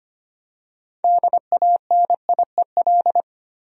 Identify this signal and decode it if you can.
Morse code, heard as a keyed tone, about 25 wpm, 700 Hz. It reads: DANIEL
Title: USA Common Names - 25wpm